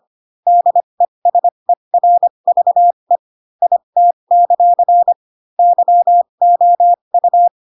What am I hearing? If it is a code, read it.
Morse code, heard as a keyed tone, about 25 wpm, spaced as for 22 wpm, 700 Hz. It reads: DESERVE IT; YOU
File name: Pinocchio / Pinocchio ch7 18 25 ik1hge